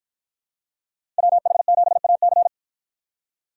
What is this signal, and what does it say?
Morse code, heard as a keyed tone, about 53 wpm, 700 Hz. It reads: WH7AC